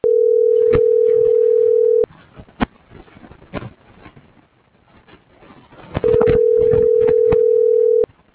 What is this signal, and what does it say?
Ringback tone (North America), 440 + 480 Hz.
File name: CENTRAL GENERAL / out deenis.chavez